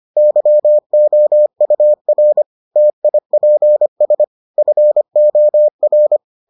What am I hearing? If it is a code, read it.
Morse code, heard as a keyed tone, about 25 wpm, 600 Hz. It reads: YOUR TIPS FOR